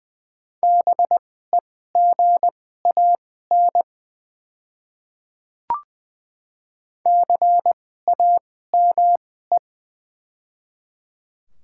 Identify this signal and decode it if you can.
Morse code, heard as a keyed tone, about 20 wpm, spaced as for 14 wpm, 700 Hz. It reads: BEGAN CAME
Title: 2x Character Spacing - Top 300 Words - 20wpm